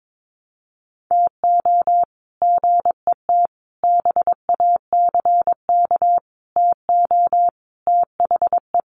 Morse code, heard as a keyed tone, about 22 wpm, 700 Hz. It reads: TO GET BACK TO THE
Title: Sets of 5 Words Encore 22wpm